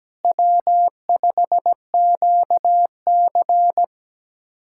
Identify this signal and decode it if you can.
Morse code, heard as a keyed tone, about 17 wpm, 700 Hz. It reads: W5QC
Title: USA Callsign DE Callsign - 4 Characters - 17wpm